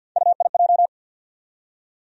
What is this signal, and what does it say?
Morse code, heard as a keyed tone, about 53 wpm, 700 Hz. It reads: UIJ
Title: USA Call Sign Postfixes - 3 Letters 50wpm